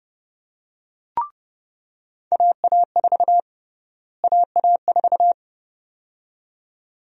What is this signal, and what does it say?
Morse code, heard as a keyed tone, about 30 wpm, 700 Hz. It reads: AA4 AA4
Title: USA Call Sign Prefixes - Repeated 2x - 30wpm